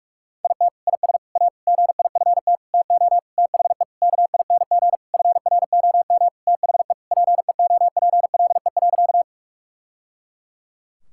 Morse code, heard as a keyed tone, about 46 wpm, 700 Hz. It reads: IT IS A GIFT TO THE KING FROM THE PEOPLE.
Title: Sentences from Top 1000 Words - Review Part 6 - 45wpm